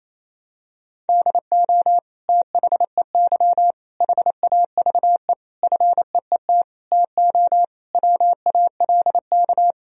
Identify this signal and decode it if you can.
Morse code, heard as a keyed tone, about 28 wpm, 700 Hz. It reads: DO THEY HAVE FEET TO WALK
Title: Sentences from Top 1000 Words - Review Part 1 - 28wpm